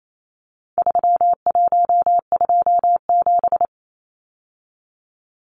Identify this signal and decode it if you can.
Morse code, heard as a keyed tone, about 28 wpm, 700 Hz. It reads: 3127
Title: Numbers - 4-Digit 28wpm